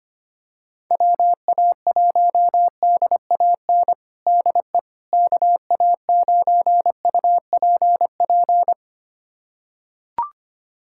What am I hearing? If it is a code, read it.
Morse code, heard as a keyed tone, about 25 wpm, 700 Hz. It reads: WA1DAN DE KA9UPP
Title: USA Callsign DE Callsign 25wpm